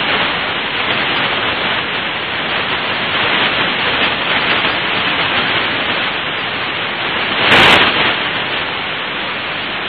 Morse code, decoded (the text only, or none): none